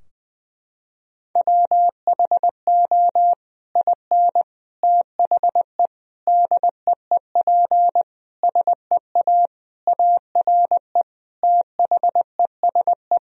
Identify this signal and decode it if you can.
Morse code, heard as a keyed tone, about 20 wpm, 700 Hz. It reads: WHO IN THE DEEP SEA ARE THESE
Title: Sentences from Top 1000 Words - Review Part 3 - 20wpm